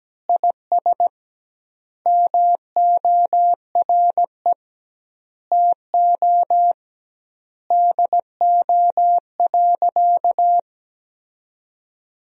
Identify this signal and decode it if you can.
Morse code, heard as a keyed tone, about 17 wpm, 700 Hz. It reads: IS MORE TO DO.
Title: 2x Word-Spacing - Sentences from Top 100 Words - 17wpm